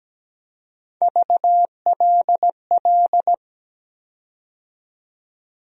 Morse code, heard as a keyed tone, about 17 wpm, 700 Hz. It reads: VLL
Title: USA Call Sign Postfixes - 3 Letters 17wpm